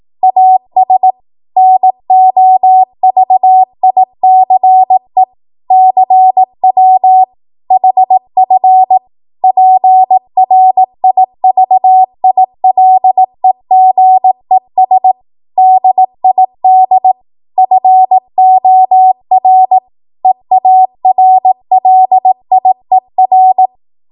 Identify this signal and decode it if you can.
Morse code, heard as a keyed tone, about 18 wpm, 750 Hz. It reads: AS NOVICE CW HF PRIVILEGES DID FOR EARLIER
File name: Morse / 18WPM